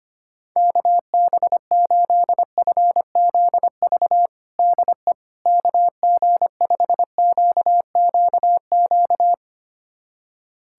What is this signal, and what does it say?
Morse code, heard as a keyed tone, about 25 wpm, 700 Hz. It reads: KB8FZV DE KG5QQQ
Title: USA Callsign DE Callsign - 6 Characters - 25wpm